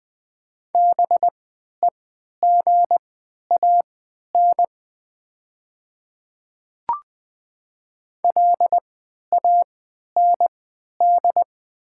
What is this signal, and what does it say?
Morse code, heard as a keyed tone, about 20 wpm, spaced as for 11 wpm, 700 Hz. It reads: BEGAN LAND
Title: 3x Character Spacing - Top 500 Words - 20wpm